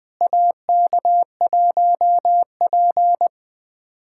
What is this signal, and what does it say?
Morse code, heard as a keyed tone, about 20 wpm, 700 Hz. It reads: AK1P
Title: USA Callsign DE Callsign - 4 Characters - 20wpm